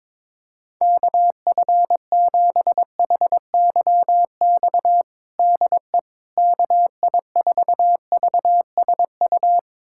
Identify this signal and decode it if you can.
Morse code, heard as a keyed tone, about 21 wpm, 700 Hz. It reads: KF7HYX DE KI4VSU
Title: USA Callsign DE Callsign - 6 Characters - 22wpm